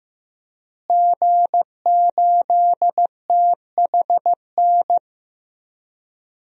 Morse code, heard as a keyed tone, about 15 wpm, 700 Hz. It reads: G8THN
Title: DXCC-100 Call Signs - 5 Characters - Repeated 2x - 15wpm